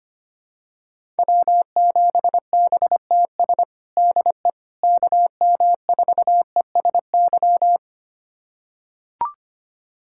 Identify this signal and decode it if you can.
Morse code, heard as a keyed tone, about 25 wpm, 700 Hz. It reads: W7BTS DE KM4ESY
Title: USA Callsign DE Callsign 25wpm